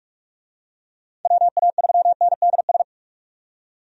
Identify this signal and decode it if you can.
Morse code, heard as a keyed tone, about 46 wpm, 700 Hz. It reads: WA3NDS
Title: USA Call Signs - 6 Characters - 45wpm